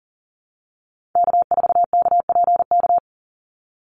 Morse code, heard as a keyed tone, about 42 wpm, 700 Hz. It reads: K4KPK
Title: NA SOTA Activator Callsigns 40wpm